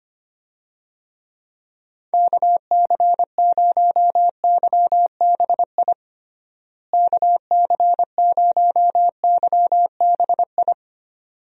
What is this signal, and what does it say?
Morse code, heard as a keyed tone, about 25 wpm, 700 Hz. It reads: KC0YBI KC0YBI
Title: USA Call Signs - 6 Characters - Repeated 2x - 25wpm